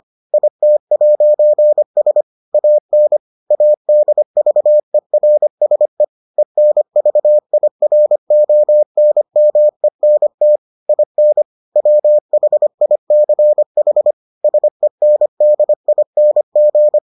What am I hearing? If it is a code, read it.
Morse code, heard as a keyed tone, about 25 wpm, 600 Hz. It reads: IT'S AN ADVERSE ENVIRONMENT IN WHICH SENDING